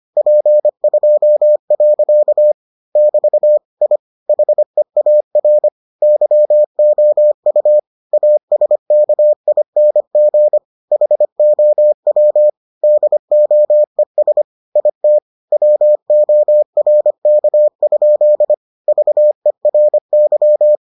Morse code, heard as a keyed tone, about 25 wpm, 600 Hz. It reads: P2. <BT> I HEAR YOU ASKING HOW DOES IT WORK? VERY